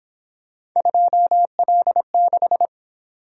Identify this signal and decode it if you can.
Morse code, heard as a keyed tone, about 27 wpm, 700 Hz. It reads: 2L6
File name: Speed-Racing - Three Letter-Number Combinations - 22wpm